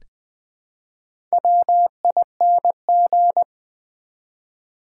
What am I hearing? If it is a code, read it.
Morse code, heard as a keyed tone, about 20 wpm, 700 Hz. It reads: WING
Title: Build A Word - 1 Letter At A Time - 4-Letter Words - 20wpm